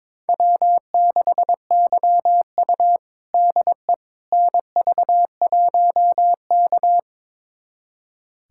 Morse code, heard as a keyed tone, about 22 wpm, 700 Hz. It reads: W6YU DE NV1K
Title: USA Callsign DE Callsign - 4 Characters - 22wpm